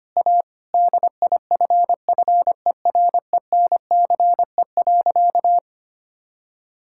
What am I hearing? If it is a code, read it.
Morse code, heard as a keyed tone, about 25 wpm, 700 Hz. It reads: A DIFFERENCE.
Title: Sentences from Top 1000 Words - Review Part 5 - 25wpm